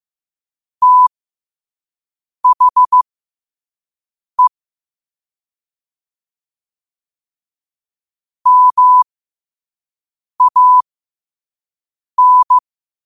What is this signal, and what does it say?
Morse code, heard as a keyed tone, about 15 wpm, spaced as for 5 wpm, 1000 Hz. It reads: THE MAN
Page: New quotes every day in morse code at 5 Words per minute.